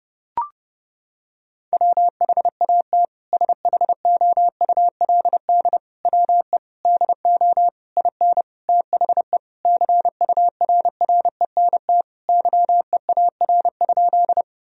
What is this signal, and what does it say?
Morse code, heard as a keyed tone, about 32 wpm, 700 Hz. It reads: WHAT SHOULD WE DO IN THE CURRENT YEAR?
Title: Sentences from Top 1000 Words - Review Part 4 - 30wpm